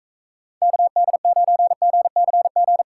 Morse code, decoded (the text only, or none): KD9GCG